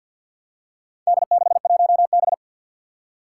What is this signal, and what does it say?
Morse code, heard as a keyed tone, about 52 wpm, 700 Hz. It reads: D61B
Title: Four Challenging Letter-Number Combinations 50wpm